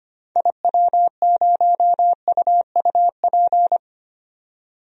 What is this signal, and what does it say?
Morse code, heard as a keyed tone, about 25 wpm, 700 Hz. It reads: IW0UUP
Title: DXCC-100 Call Signs - 6 Characters - 25wpm